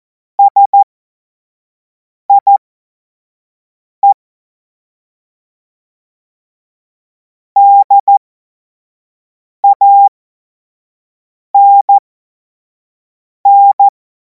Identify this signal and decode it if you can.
Morse code, heard as a keyed tone, about 14 wpm, spaced as for 5 wpm, 800 Hz. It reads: SIE DANN
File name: Morsekurs / Witze / Witz 12. Mai 2026 25 F70